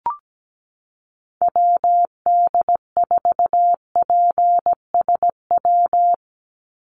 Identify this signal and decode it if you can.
Morse code, heard as a keyed tone, about 17 wpm, 700 Hz. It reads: WD4PSW
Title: USA Call Signs 17wpm